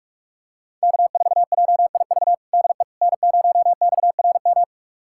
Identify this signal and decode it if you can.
Morse code, heard as a keyed tone, about 41 wpm, 700 Hz. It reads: K4JIV DE N0XRM
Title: USA Callsign DE Callsign - 5 Characters - 45wpm